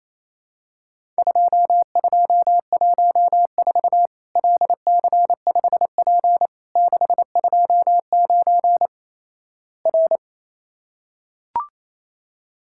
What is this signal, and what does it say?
Morse code, heard as a keyed tone, about 27 wpm, 700 Hz. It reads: 2214 LC5P 629 R
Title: Worked All Europe DX Contest - QTC Exchange - 28wpm